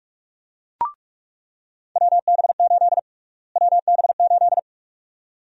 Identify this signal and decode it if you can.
Morse code, heard as a keyed tone, about 47 wpm, 700 Hz. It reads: WB8 WB8
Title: USA Call Sign Prefixes - Repeated 2x - 45wpm